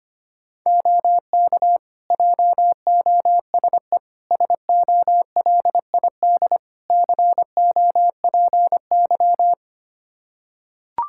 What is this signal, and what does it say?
Morse code, heard as a keyed tone, about 25 wpm, 700 Hz. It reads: OK JOSE SOLID COPY
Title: K1USN SST QSO Practice 25wpm